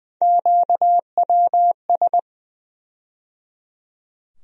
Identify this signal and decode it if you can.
Morse code, heard as a keyed tone, about 20 wpm, 700 Hz. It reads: QWS
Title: USA Call Sign Postfixes - 3 Letters 20wpm